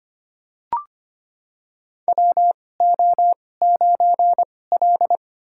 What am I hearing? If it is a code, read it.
Morse code, heard as a keyed tone, about 25 wpm, spaced as for 18 wpm, 700 Hz. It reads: WO9L
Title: USA Call Signs - 4 Characters - 2x Character Spacing - 25wpm